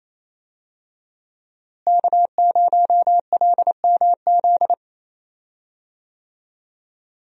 Morse code, heard as a keyed tone, about 28 wpm, 700 Hz. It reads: K0LMZ
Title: USA Call Signs - 5 Characters - Repeated 3x - 28wpm